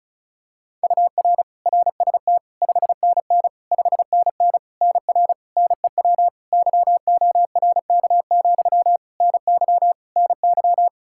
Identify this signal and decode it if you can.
Morse code, heard as a keyed tone, about 36 wpm, 700 Hz. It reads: UR RST 5NN 5NN NR NEW YORK, NY NY
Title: K1USN SST QSO Practice 35wpm